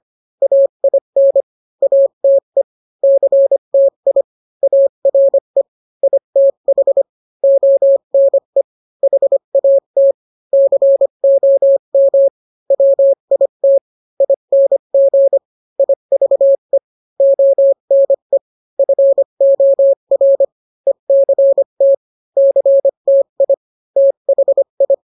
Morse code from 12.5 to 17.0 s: WIT ING IVE